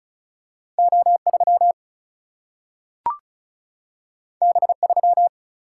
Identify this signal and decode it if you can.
Morse code, heard as a keyed tone, about 36 wpm, 700 Hz. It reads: O3 B3
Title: Single-Letter Single-Number 35wpm